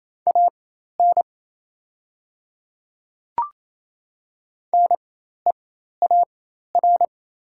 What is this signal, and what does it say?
Morse code, heard as a keyed tone, about 28 wpm, spaced as for 13 wpm, 690 Hz. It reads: AN NEAR
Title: 4x Character Spacing - Top 200 Words - 28wpm